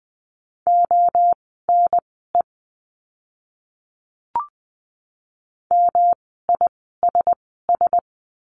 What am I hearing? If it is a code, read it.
Morse code, heard as a keyed tone, about 20 wpm, spaced as for 14 wpm, 700 Hz. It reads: ONE MISS
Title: 108 Lesson 18f - Letter m - Words 20wpm